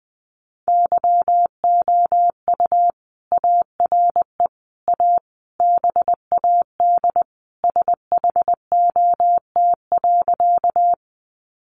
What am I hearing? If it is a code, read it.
Morse code, heard as a keyed tone, about 20 wpm, 700 Hz. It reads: YOU ARE A BAD SHOT.
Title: Sentences from Top 1100 Words 20wpm